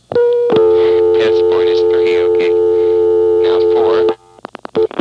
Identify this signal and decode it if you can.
Dial tone (North America), 350 + 440 Hz, with speech.
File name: MoreheadCityNCNoTest